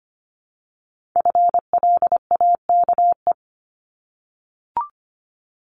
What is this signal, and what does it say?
Morse code, heard as a keyed tone, about 25 wpm, 700 Hz. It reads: FLAKE
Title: Multi-letter Phonemes Repeated 3x 25wpm